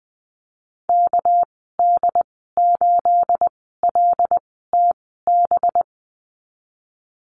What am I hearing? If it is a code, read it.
Morse code, heard as a keyed tone, about 20 wpm, spaced as for 15 wpm, 700 Hz. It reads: KD8LTB
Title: 219 Lesson 35g - Number 0 - Call Signs 20wpm